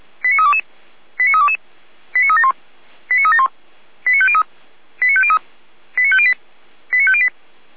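Five-tone selective call (ZVEI), heard as quick runs of five tones, tones of about 70 ms. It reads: 89322 89322 89371 89371 89503 89503 89508 89508
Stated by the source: ZVEI five-tone selective call 89305 received via SDR, from the audio.f85360.feuerwehr-orlamuende.org fire brigade server.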